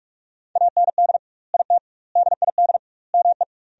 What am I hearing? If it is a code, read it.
Morse code, heard as a keyed tone, about 43 wpm, 700 Hz. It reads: AND IT DID ME
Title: 2x Word-Spacing - Sentences from Top 200 Words - 45wpm